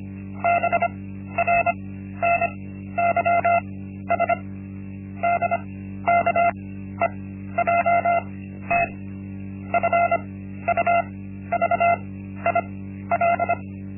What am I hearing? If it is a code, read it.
Morse code, heard as a keyed tone, about 25 wpm, spaced as for 13 wpm, 690 Hz. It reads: BRNYS DKEJT FUVIL
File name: Linksmakalnis-1